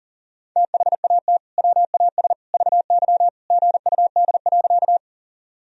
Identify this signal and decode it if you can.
Morse code, heard as a keyed tone, about 40 wpm, 700 Hz. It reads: THAT WAS VY GUD.
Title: Sentences from Top 200 Words and Abbreviations 40wpm